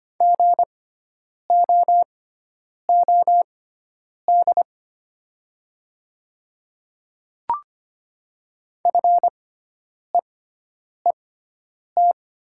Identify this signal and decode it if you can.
Morse code, heard as a keyed tone, about 25 wpm, spaced as for 9 wpm, 700 Hz. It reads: GOOD FEET
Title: 6x Character Spacing - Top 500 Words - 25wpm